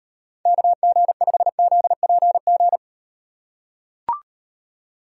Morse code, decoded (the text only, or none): KG5ZPG